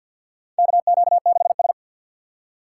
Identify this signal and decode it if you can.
Morse code, heard as a keyed tone, about 50 wpm, 700 Hz. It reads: KQ6S
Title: USA Call Signs - 6x Character Spacing - 50wpm